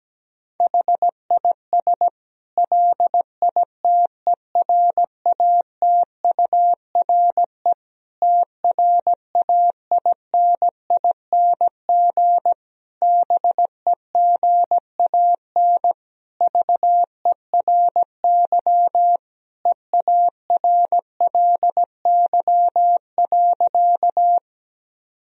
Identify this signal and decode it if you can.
Morse code, heard as a keyed tone, about 17 wpm, 700 Hz. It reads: HIS LITERATURE TRAINING BEGAN VERY EARLY.
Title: Sentences from Top 1000 Words - Review Part 4 - 17wpm